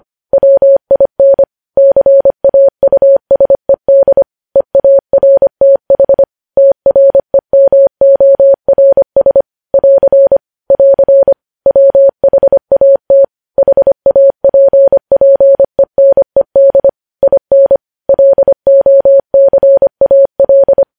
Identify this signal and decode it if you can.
Morse code, heard as a keyed tone, about 25 wpm, 550 Hz. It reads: WIN CAUSED EARTH TREMORS <AR> <AR> WHAT HAPPENED IN LOCAL
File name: podcast / News headlines-1x-25x25